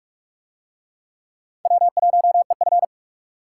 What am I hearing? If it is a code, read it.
Morse code, heard as a keyed tone, about 42 wpm, 700 Hz. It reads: W1EF